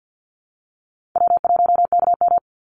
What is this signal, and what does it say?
Morse code, heard as a keyed tone, about 50 wpm, 700 Hz. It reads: W1KM
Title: Easy Call Signs - 50wpm